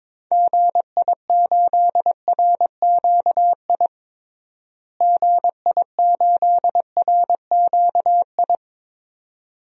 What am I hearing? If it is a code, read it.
Morse code, heard as a keyed tone, about 22 wpm, 700 Hz. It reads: GI8RQI GI8RQI
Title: DXCC-100 Call Signs - 6 Characters - Repeated 2x - 22wpm